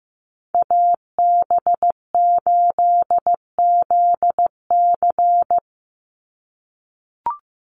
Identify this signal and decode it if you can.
Morse code, heard as a keyed tone, about 15 wpm, 700 Hz. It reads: AB8ZC
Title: USA Call Signs 15wpm